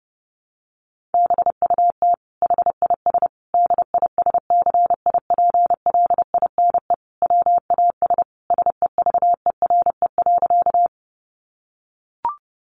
Morse code, heard as a keyed tone, about 30 wpm, 700 Hz. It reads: BUT HIS DISCIPLINE WAS SEVERE.
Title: Sentences from Top 1400 Words 30wpm